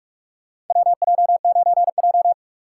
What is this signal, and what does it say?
Morse code, heard as a keyed tone, about 43 wpm, 690 Hz. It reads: WJ9J
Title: USA Call Signs - 4 Characters - 3x Character Spacing - 45wpm